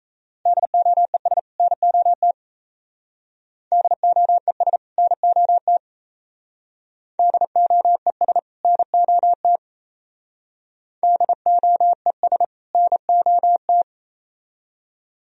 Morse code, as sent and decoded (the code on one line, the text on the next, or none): -.. --- . ... / -. --- - / -.. --- . ... / -. --- - / -.. --- . ... / -. --- - / -.. --- . ... / -. --- -
DOES NOT DOES NOT DOES NOT DOES NOT